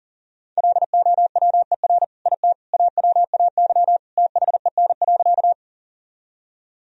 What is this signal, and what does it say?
Morse code, heard as a keyed tone, about 40 wpm, 700 Hz. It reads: LOWER IT AWAY THEN.
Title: Sentences from Top 1000 Words - Review Part 4 - 40wpm